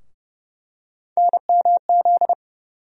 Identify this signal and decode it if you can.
Morse code, heard as a keyed tone, about 30 wpm, 700 Hz. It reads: NMZ
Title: USA Call Sign Postfixes - 3 Letters 30wpm